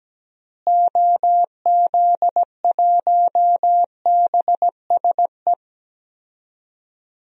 Morse code, heard as a keyed tone, about 17 wpm, 700 Hz. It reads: OZ1BSE
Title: DXCC-100 Call Signs - 6 Characters - Repeated 3x - 17wpm